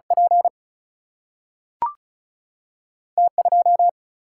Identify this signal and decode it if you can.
Morse code, heard as a keyed tone, about 36 wpm, 700 Hz. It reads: P T 2
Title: Single-Letter Single-Number 35wpm